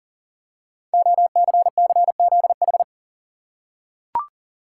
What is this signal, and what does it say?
Morse code, heard as a keyed tone, about 39 wpm, 690 Hz. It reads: OCCZH
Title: Code Groups - 5 Letters - 40wpm